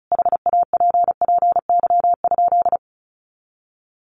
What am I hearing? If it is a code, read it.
Morse code, heard as a keyed tone, about 34 wpm, 700 Hz. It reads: HAPPY?
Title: Sentences from Top 1100 Words 35wpm